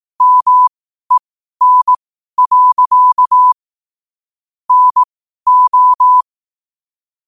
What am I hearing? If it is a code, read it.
Morse code, heard as a keyed tone, about 18 wpm, spaced as for 12 wpm, 1000 Hz. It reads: MEN. NO
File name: qotdfiles / 12wpm